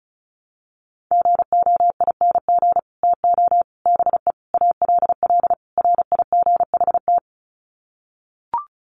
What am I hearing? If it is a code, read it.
Morse code, heard as a keyed tone, about 34 wpm, 700 Hz. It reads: GOING TO BE ALL RIGHT
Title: Sets of 5 Words Encore2 35wpm